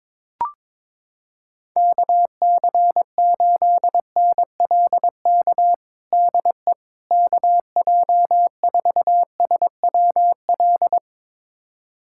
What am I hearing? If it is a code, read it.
Morse code, heard as a keyed tone, about 22 wpm, 700 Hz. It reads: KC8NLK DE KJ4SWL
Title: USA Callsign DE Callsign 22wpm